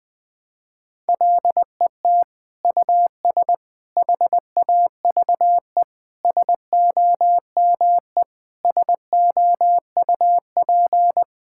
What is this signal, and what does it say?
Morse code, heard as a keyed tone, about 20 wpm, 700 Hz. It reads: LET US HAVE SOME SOUP
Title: Sentences from Top 1000 Words - Review Part 6 - 20wpm